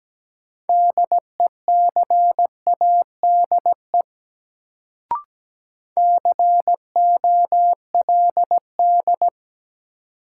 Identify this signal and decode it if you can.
Morse code, heard as a keyed tone, about 17 wpm, 700 Hz. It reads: DECADE COLD
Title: Sets of 5 Words - New Word Drill - 17wpm